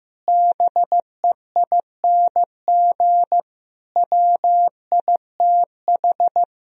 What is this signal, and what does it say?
Morse code, heard as a keyed tone, about 15 wpm, 700 Hz. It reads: BEING WITH
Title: Speed-Racing - Sets of 5 Words 15wpm